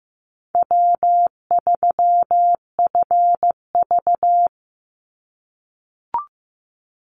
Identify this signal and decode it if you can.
Morse code, heard as a keyed tone, about 15 wpm, 700 Hz. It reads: W3FV
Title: Easy Call Signs - 15wpm